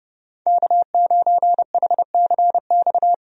Morse code, heard as a keyed tone, about 30 wpm, 700 Hz. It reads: K9HCX
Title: USA Call Signs - 5 Characters - 30wpm